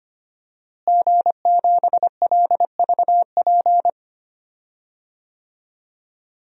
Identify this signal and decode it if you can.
Morse code, heard as a keyed tone, about 25 wpm, 700 Hz. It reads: G7LVP
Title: DXCC-100 Call Signs - 5 Characters - Repeated 2x - 25wpm